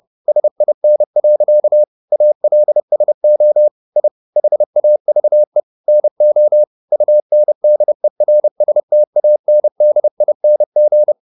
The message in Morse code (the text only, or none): SIN. ALSO I HAVE NO UNDERSTANDING